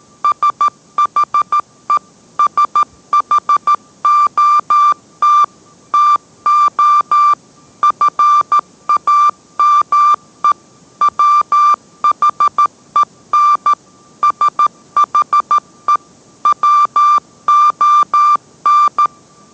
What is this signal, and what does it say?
Morse code, heard as a keyed tone, about 13 wpm, 1180 Hz. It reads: SHE SHOT TO FAME WHEN SHE WON